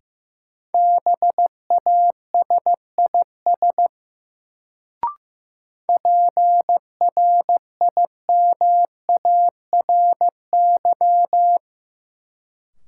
Morse code, heard as a keyed tone, about 15 wpm, 700 Hz. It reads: BASIS PRIMARY
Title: Sentences from Top 1000 Words - Review Part 5 - New Word Drill - 15wpm